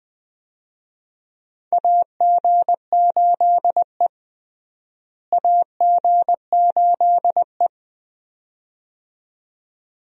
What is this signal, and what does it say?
Morse code, heard as a keyed tone, about 20 wpm, 700 Hz. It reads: AG8E AG8E
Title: USA Call Signs - 4 Characters - Repeated 2x - 20wpm